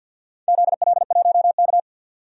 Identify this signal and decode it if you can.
Morse code, heard as a keyed tone, about 51 wpm, 700 Hz. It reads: ZL1K
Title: DXCC-100 Call Signs - 4 Characters - 50wpm